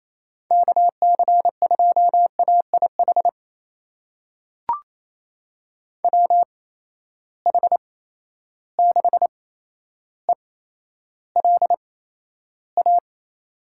Morse code, heard as a keyed tone, about 28 wpm, spaced as for 12 wpm, 700 Hz. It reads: KC2AIH WH6ELA
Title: USA Call Signs - 6 Characters - 8x Character Spacing - 28wpm